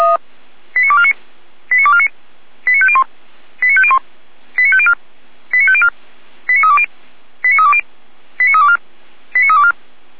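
Five-tone selective call (ZVEI), heard as quick runs of five tones, tones of about 70 ms. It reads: 89249 89249 89501 89501 89504 89504 89320 89320 89324 89324
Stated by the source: ZVEI five-tone call received on an SDR, from audio.f85360.feuerwehr-orlamuende.org (Feuerwehr Orlamünde fire service).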